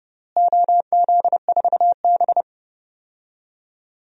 Morse code, heard as a keyed tone, about 30 wpm, 700 Hz. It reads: OZ4B
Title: DXCC-100 Call Signs - 4 Characters - Repeated 2x - 30wpm